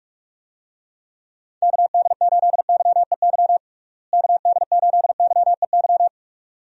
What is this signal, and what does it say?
Morse code, heard as a keyed tone, about 46 wpm, 700 Hz. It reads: KD8YEY KD8YEY
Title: USA Call Signs - 6 Characters - Repeated 2x - 45wpm